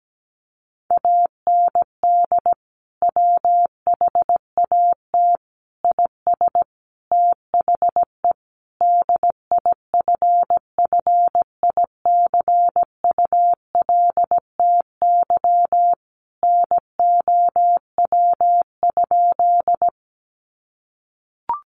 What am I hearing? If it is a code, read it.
Morse code, heard as a keyed tone, about 17 wpm, 700 Hz. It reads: AND WHAT IS THE DIFFICULTY NOW?
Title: Sentences from Top 1500 Words 17wpm